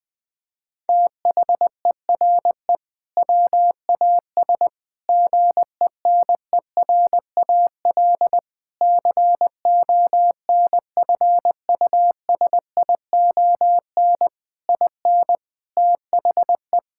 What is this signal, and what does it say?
Morse code, heard as a keyed tone, about 20 wpm, 700 Hz. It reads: THERE WAS GENERAL CONFUSION IN THE
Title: Sentences from Top 1600 Words 20wpm